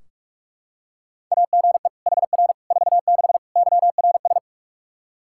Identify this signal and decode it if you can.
Morse code, heard as a keyed tone, about 46 wpm, 700 Hz. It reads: AGE HR 46 YRS
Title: Practice QSO - Exchange 3 - More Info - 45wpm